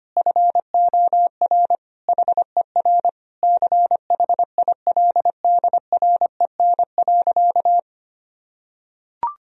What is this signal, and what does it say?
Morse code, heard as a keyed tone, about 25 wpm, 700 Hz. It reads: FOR HER CHILDREN.
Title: Sentences from Top 1000 Words - Review Part 3 - 25wpm